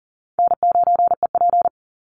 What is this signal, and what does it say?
Morse code, heard as a keyed tone, about 39 wpm, 700 Hz. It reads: N9EP
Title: Easy Call Signs - 40wpm